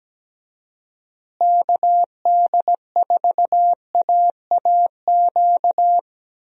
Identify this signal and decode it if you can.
Morse code, heard as a keyed tone, about 17 wpm, 700 Hz. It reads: KD4AAQ
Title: USA Call Signs - 6 Characters - Repeated 3x - 17wpm